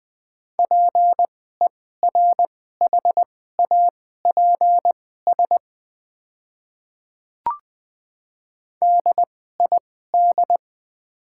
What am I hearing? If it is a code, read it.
Morse code, heard as a keyed tone, about 20 wpm, spaced as for 14 wpm, 700 Hz. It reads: PERHAPS DID
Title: 2x Character Spacing - Top 500 Words - 20wpm